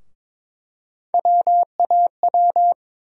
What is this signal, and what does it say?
Morse code, heard as a keyed tone, about 22 wpm, 700 Hz. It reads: WAW
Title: USA Call Sign Postfixes - 3 Letters 22wpm